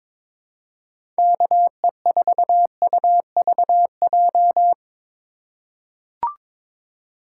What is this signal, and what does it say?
Morse code, heard as a keyed tone, about 22 wpm, 700 Hz. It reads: KE4UVJ
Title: USA Call Signs - 6 Characters - 3x Character Spacing - 22wpm